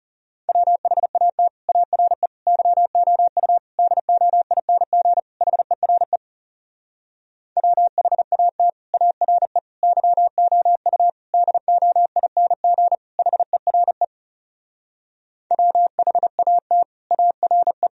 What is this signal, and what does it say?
Morse code, heard as a keyed tone, about 37 wpm, 700 Hz. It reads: WHAT ARE YOU DOING HERE WHAT ARE YOU DOING HERE WHAT ARE
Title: Speed-Racing - Sets of 5 Words 30wpm